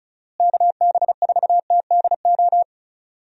Morse code, heard as a keyed tone, about 34 wpm, 700 Hz. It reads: KB4TDO
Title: USA Call Signs - 6 Characters - Repeated 3x - 35wpm